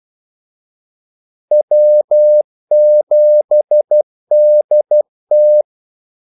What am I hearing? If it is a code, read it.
Morse code, heard as a keyed tone, about 12 wpm, 600 Hz. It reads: W7DT
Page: 2 ekstra ordmellemrum mellem kaldesignalerne